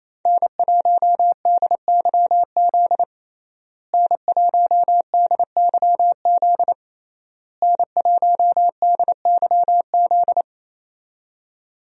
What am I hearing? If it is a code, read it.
Morse code, heard as a keyed tone, about 28 wpm, 700 Hz. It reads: N1DYZ N1DYZ N1DYZ
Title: USA Call Signs - 5 Characters - Repeated 3x - 28wpm